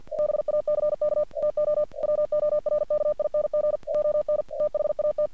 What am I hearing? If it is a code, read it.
Morse code, heard as a keyed tone, about 48 wpm, 600 Hz. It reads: BACK TO WORKING ON THAT
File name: archived / CRSnet